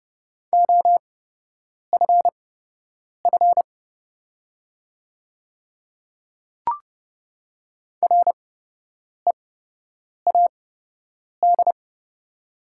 Morse code, heard as a keyed tone, about 29 wpm, spaced as for 8 wpm, 700 Hz. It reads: OFF READ
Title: 8x Character Spacing - Top 200 Words - 30wpm